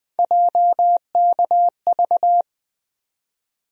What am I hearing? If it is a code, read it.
Morse code, heard as a keyed tone, about 21 wpm, 700 Hz. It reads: JKV
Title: USA Call Sign Postfixes - 3 Letters 20wpm